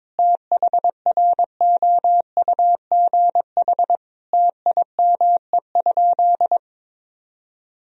Morse code, heard as a keyed tone, about 22 wpm, 700 Hz. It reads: THROUGH TIME?
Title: Sentences from Top 1000 Words - Review Part 2 - 22wpm